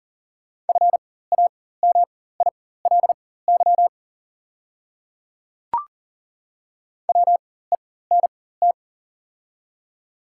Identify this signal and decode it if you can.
Morse code, heard as a keyed tone, about 38 wpm, spaced as for 18 wpm, 690 Hz. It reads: FAMILY WENT
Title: 4x Character Spacing - Top 500 Words - 40wpm